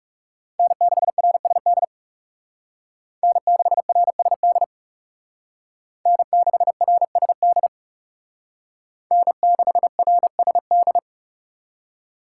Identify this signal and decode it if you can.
Morse code, heard as a keyed tone, about 38 wpm, 700 Hz. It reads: N6RSD N6RSD N6RSD N6RSD